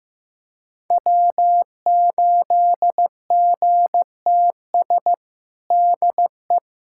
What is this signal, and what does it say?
Morse code, heard as a keyed tone, about 15 wpm, 700 Hz. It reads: W8GTS DE
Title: USA Callsign DE Callsign - 5 Characters - 15wpm